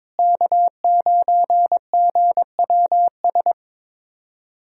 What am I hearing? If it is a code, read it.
Morse code, heard as a keyed tone, about 22 wpm, 700 Hz. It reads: K9GWS
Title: K1USN SST QSO Practice - 22wpm